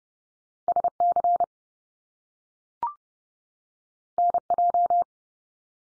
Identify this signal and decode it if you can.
Morse code, heard as a keyed tone, about 30 wpm, 700 Hz. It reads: SC NJ
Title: US States 30WPM